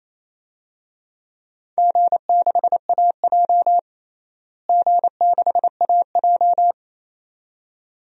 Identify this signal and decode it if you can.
Morse code, heard as a keyed tone, about 27 wpm, 700 Hz. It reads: G6AJ G6AJ
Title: DXCC-100 Call Signs - 4 Characters - Repeated 2x - 28wpm